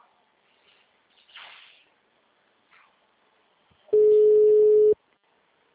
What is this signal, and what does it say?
Ringback tone (Europe, 425 Hz).
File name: out- -2013- - - .58293